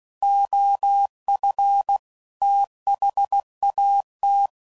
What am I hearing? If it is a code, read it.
Morse code, heard as a keyed tone, about 16 wpm, 800 Hz. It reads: OF THAT
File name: CW Code Groups @15WPM / Technical Subjects / PwrSupply